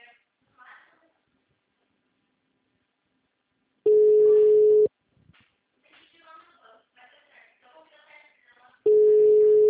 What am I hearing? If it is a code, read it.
Ringback tone (Europe, 425 Hz), with speech.